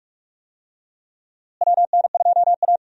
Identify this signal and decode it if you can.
Morse code, heard as a keyed tone, about 48 wpm, 700 Hz. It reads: WN2A
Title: USA Call Signs - 4 Characters - Repeated 2x - 45wpm